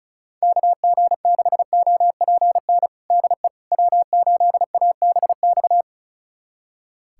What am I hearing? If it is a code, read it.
Morse code, heard as a keyed tone, about 36 wpm, 700 Hz. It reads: KG6OPN DE W8ABX
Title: USA Callsign DE Callsign 35wpm